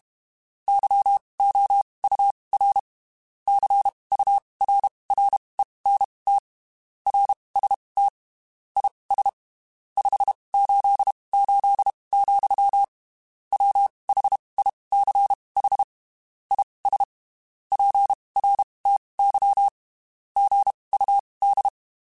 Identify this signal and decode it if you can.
Morse code, heard as a keyed tone, about 32 wpm, spaced as for 23 wpm, 800 Hz. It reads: YOUR CURRENT RST IS 588, WHICH IS PRTY GUD
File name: (spaced) / QSOs Spaced-mp3s / WA7QOX 1 @30 sp